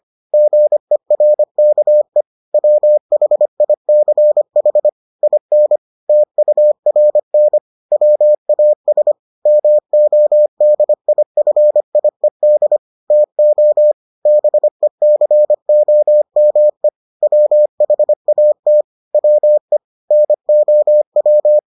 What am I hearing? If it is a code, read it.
Morse code, heard as a keyed tone, about 25 wpm, 600 Hz. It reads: GERKE WHICH IN TURN WAS MODIFIED TO BECOME WHAT WE NOW